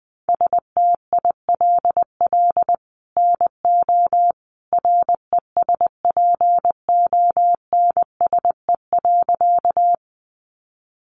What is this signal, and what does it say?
Morse code, heard as a keyed tone, about 20 wpm, 700 Hz. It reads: STILL NO RESPONSE.
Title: Sentences from Top 1400 Words 20wpm